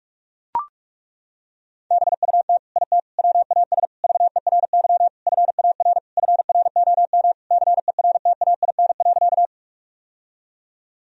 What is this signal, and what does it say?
Morse code, heard as a keyed tone, about 41 wpm, 700 Hz. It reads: BUT IT WAS VERY FAR FROM CERTAIN.
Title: Sentences from Top 1000 Words - Review Part 2 - 45wpm